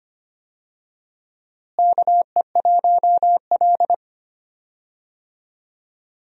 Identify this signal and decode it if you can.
Morse code, heard as a keyed tone, about 25 wpm, 700 Hz. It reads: KE1L
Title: USA Call Signs - 4 Characters - Repeated 2x - 25wpm